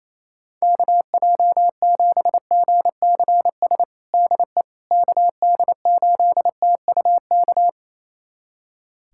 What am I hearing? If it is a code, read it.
Morse code, heard as a keyed tone, about 28 wpm, 700 Hz. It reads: KJ7GCS DE KD8TUK
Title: USA Callsign DE Callsign 28wpm